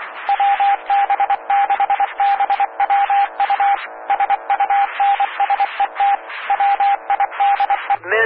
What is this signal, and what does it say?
Morse code, heard as a keyed tone, about 23 wpm, 800 Hz, with radio noise. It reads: WB6BWU SUNSET WIDE
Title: Claremont, CA Police, Amateur Radio